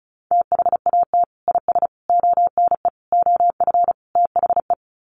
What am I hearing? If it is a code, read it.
Morse code, heard as a keyed tone, about 34 wpm, 700 Hz. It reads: THAT IS ONE OF THE
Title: Sets of 5 Words Encore 35wpm